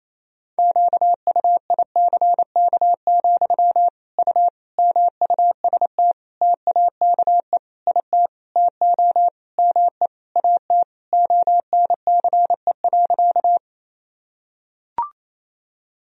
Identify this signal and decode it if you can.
Morse code, heard as a keyed tone, about 28 wpm, 700 Hz. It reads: QUICK, U MUST TAKE IT TO ME AT ONCE.
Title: Sentences from Top 500 Words and Abbreviations 28wpm